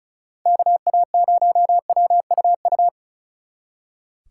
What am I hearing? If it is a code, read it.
Morse code, heard as a keyed tone, about 45 wpm, 700 Hz. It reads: KA0WUU